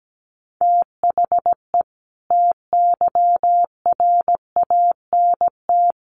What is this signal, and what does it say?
Morse code, heard as a keyed tone, about 17 wpm, 700 Hz. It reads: THE TYRANT
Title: A Sound Of Thunder 17wpm